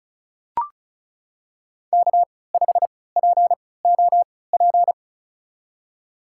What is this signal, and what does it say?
Morse code, heard as a keyed tone, about 36 wpm, spaced as for 20 wpm, 700 Hz. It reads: K5POP
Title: USA Call Signs - 5 Characters - 3x Character Spacing - 35wpm